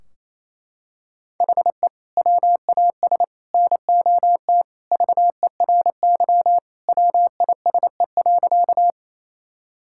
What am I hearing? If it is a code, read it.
Morse code, heard as a keyed tone, about 28 wpm, 700 Hz. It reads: HE WAS NOT VERY WISE.
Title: Sentences from Top 1600 Words 28wpm